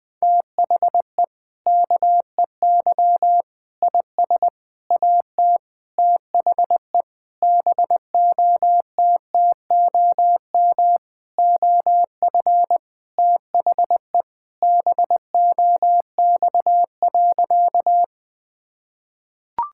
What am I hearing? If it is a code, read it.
Morse code, heard as a keyed tone, about 20 wpm, 700 Hz. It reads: THE KEY IS AT THE BOTTOM OF THE BOX.
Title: Sentences with 5-Word Phrases - 20wpm